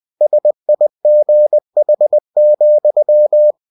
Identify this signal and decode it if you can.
Morse code, heard as a keyed tone, about 20 wpm, 600 Hz. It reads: SIGH,